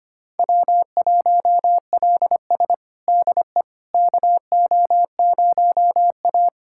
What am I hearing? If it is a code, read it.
Morse code, heard as a keyed tone, about 25 wpm, 700 Hz. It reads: W1LS DE KO0A
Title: USA Callsign DE Callsign - 4 Characters - 25wpm